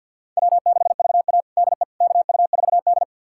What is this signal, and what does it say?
Morse code, heard as a keyed tone, about 46 wpm, 700 Hz. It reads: W6VA DE KU4D
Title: USA Callsign DE Callsign - 4 Characters - 50wpm